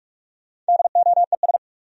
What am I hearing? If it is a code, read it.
Morse code, heard as a keyed tone, about 58 wpm, 690 Hz. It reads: DOES